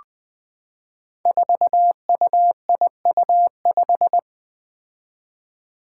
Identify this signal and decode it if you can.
Morse code, heard as a keyed tone, about 20 wpm, 700 Hz. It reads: 4UIU5
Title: Five Challenging Letter-Number Combinations 20wpm